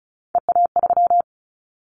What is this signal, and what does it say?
Morse code, heard as a keyed tone, about 34 wpm, 700 Hz. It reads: EA3
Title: Callsign Prefixes for Top 25 DXCC - 35wpm